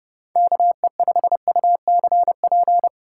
Morse code, heard as a keyed tone, about 31 wpm, 700 Hz. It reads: KE5UCP